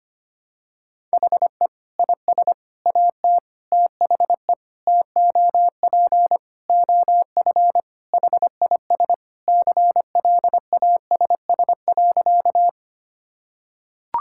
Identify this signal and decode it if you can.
Morse code, heard as a keyed tone, about 25 wpm, 700 Hz. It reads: HE IS AT THE TOP OF HIS CLASS.
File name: practice / 25wpm / Sentences with 4-Word Phrases - 25wpm